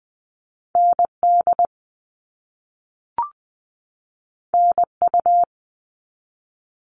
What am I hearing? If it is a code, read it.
Morse code, heard as a keyed tone, about 20 wpm, 700 Hz. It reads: ND NU
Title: All Cut Numbers - 2-Digit 20wpm